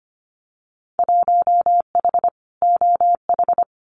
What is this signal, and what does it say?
Morse code, heard as a keyed tone, about 25 wpm, 700 Hz. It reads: 1H OH
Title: Winter Field Day Exchanges 25wpm